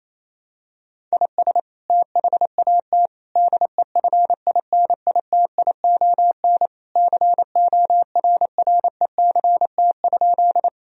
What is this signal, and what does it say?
Morse code, heard as a keyed tone, about 28 wpm, 700 Hz. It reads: IS THAT DEFINITION CORRECT?
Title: Sentences from Top 1000 Words - Review Part 4 - 28wpm